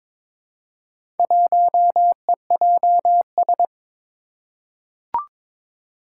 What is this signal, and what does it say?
Morse code, heard as a keyed tone, about 22 wpm, 700 Hz. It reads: 1EJS
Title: Four Challenging Letter-Number Combinations 22wpm